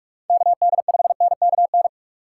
Morse code, heard as a keyed tone, about 41 wpm, 700 Hz. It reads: KD5NKN
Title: USA Call Signs - 6 Characters - 4x Character Spacing - 45wpm